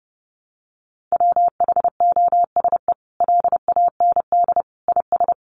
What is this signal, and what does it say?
Morse code, heard as a keyed tone, about 30 wpm, 700 Hz. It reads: WHOSE LAND IS
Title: Sentences from Top 1200 Words 30wpm